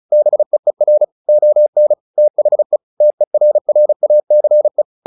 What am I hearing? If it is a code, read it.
Morse code, heard as a keyed tone, about 34 wpm, 600 Hz. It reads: BEER ON THE TERRACE
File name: the old man and the sea / the old man and the sea 1 35wpm